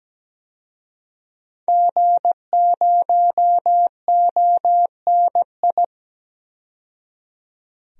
Morse code, heard as a keyed tone, about 17 wpm, 700 Hz. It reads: G0ONI
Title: DXCC-100 Call Signs - 5 Characters - 17wpm